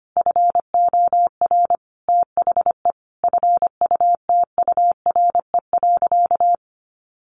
Morse code, heard as a keyed tone, about 25 wpm, 700 Hz. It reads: FOR THE FUTURE.
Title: Sentences from Top 1400 Words 25wpm